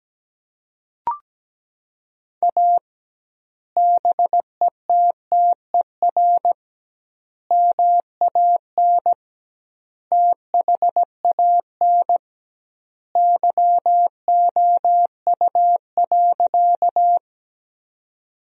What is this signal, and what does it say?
Morse code, heard as a keyed tone, about 17 wpm, 700 Hz. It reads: A BETTER MAN THAN YOU.
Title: 2x Word-Spacing - Sentences from Top 400 Words - 17wpm